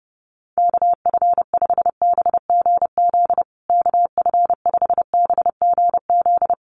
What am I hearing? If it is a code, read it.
Morse code, heard as a keyed tone, about 30 wpm, 700 Hz. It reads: KF5BGZ KF5BGZ
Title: USA Call Signs Repeated 3x 30wpm